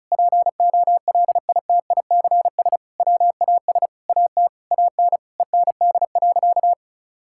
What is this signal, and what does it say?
Morse code, heard as a keyed tone, about 36 wpm, 700 Hz. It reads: POLITICS WAS AT AN END.
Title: Sentences from Top 1000 Words - Review Part 6 - 35wpm